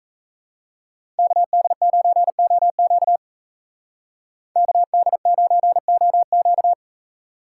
Morse code, heard as a keyed tone, about 34 wpm, 700 Hz. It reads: KD9OQ KD9OQ
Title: Speed Racing - USA 2x2 Call Signs 28wpm